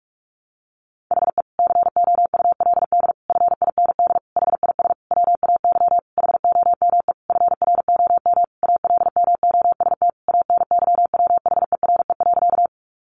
Morse code, heard as a keyed tone, about 44 wpm, 700 Hz. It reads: HE COULD FIND HIS WAY HOME FROM ALMOST ANYWHERE.
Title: Sentences from Top 1400 Words 45wpm